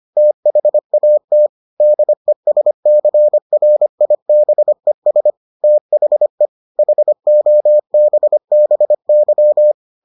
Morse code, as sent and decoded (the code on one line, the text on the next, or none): - .... .- - / -.. ... -.-. .-. .. -... ... / - .... / .... --- -... -... -.--
THAT DESCRIBES THE HOBBY